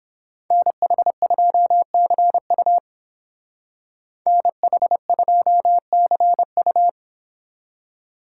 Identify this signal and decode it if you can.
Morse code, heard as a keyed tone, about 29 wpm, 700 Hz. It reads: NH2CU NH2CU